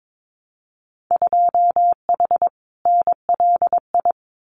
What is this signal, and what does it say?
Morse code, heard as a keyed tone, about 22 wpm, 700 Hz. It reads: 2H NLI
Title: Winter Field Day Exchanges Encore 22wpm